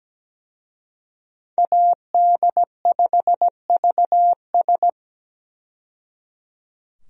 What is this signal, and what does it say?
Morse code, heard as a keyed tone, about 17 wpm, 700 Hz. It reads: AD5VS
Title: USA Call Signs - 5 Characters - 17wpm